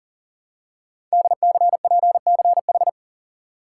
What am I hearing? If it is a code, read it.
Morse code, heard as a keyed tone, about 38 wpm, 690 Hz. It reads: DCPCH
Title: Code Groups - 5 Letters - 40wpm